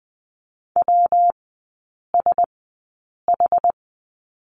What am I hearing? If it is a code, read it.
Morse code, heard as a keyed tone, about 20 wpm, spaced as for 8 wpm, 700 Hz. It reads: WSH
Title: 114 Lesson 19e - Letter w - Three Characters 20wpm